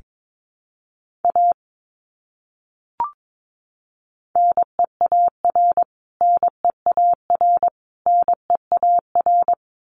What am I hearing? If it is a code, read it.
Morse code, heard as a keyed tone, about 22 wpm, 700 Hz. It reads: A NEAR NEAR NEAR
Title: Top 200 Words Repeated 22wpm